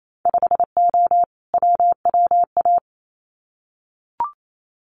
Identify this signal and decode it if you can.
Morse code, heard as a keyed tone, about 28 wpm, 700 Hz. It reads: 5O WWA
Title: Winter Field Day Exchanges Encore 28wpm